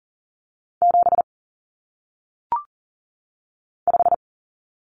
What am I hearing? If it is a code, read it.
Morse code, heard as a keyed tone, about 39 wpm, 700 Hz. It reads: 7 5
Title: Numbers - 1-Digit 40wpm